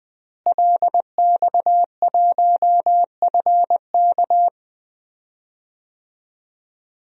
Morse code, heard as a keyed tone, about 20 wpm, 700 Hz. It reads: LX1FK
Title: DXCC-100 Call Signs - 5 Characters - Repeated 2x - 20wpm